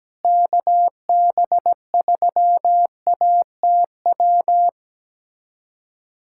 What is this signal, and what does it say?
Morse code, heard as a keyed tone, about 17 wpm, 700 Hz. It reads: KB3ATW
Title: USA Callsign DE Callsign 17wpm